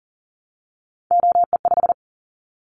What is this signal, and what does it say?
Morse code, heard as a keyed tone, about 38 wpm, 700 Hz. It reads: OE5
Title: Callsign Prefixes for Top 50 DXCC - 40wpm